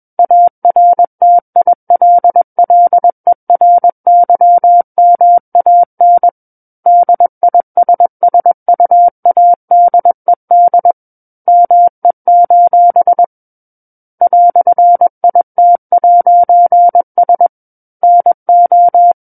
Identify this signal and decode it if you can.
Morse code, heard as a keyed tone, about 21 wpm, 700 Hz. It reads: ARTILLERYMAN DISSUADED ME: "IT'S NO
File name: Book One / 12-Chapter 12 - 21 WPM